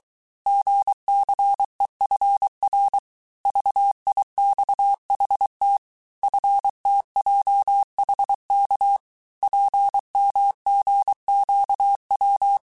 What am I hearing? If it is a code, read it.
Morse code, heard as a keyed tone, about 23 wpm, 800 Hz. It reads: GCEFR VIXHT FTJHK PMGQW
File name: cours-du-mercredi / audios / cours-855-texte-chiffre-vitesse-22-mpm